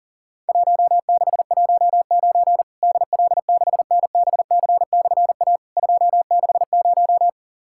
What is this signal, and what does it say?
Morse code, heard as a keyed tone, about 40 wpm, 700 Hz. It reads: 1619 DL6NBC/A 260
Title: Worked All Europe DX Contest - QTC Exchange - 40wpm